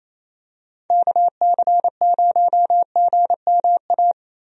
Morse code, heard as a keyed tone, about 28 wpm, 700 Hz. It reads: KC0GMA
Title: USA Call Signs - 6 Characters - Repeated 2x - 28wpm